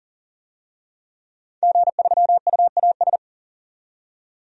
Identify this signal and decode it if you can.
Morse code, heard as a keyed tone, about 38 wpm, 700 Hz. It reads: G3UAS
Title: DXCC-100 Call Signs - 5 Characters - 40wpm